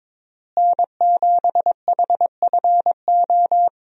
Morse code, heard as a keyed tone, about 22 wpm, 700 Hz. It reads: N7HFO
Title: USA Call Signs - 5 Characters - Repeated 3x - 22wpm